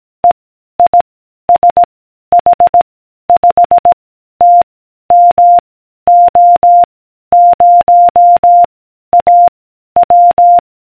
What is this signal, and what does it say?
Morse code, heard as a keyed tone, about 18 wpm, 700 Hz. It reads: E I S H 5 T M O 0 A W